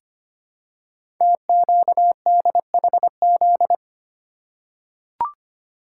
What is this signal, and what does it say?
Morse code, heard as a keyed tone, about 25 wpm, 700 Hz. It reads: TQDHZ
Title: Code Groups - 5 Letter:Numbers - 25wpm